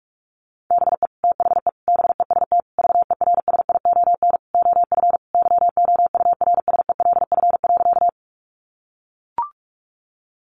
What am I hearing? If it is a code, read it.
Morse code, heard as a keyed tone, about 44 wpm, 700 Hz. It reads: BE THE BEST VERSION OF YOURSELF.
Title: Sentences from Top 1400 Words 45wpm